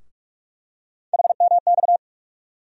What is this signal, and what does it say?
Morse code, heard as a keyed tone, about 45 wpm, 700 Hz. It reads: HMX
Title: USA Call Sign Postfixes - 3 Letters 45wpm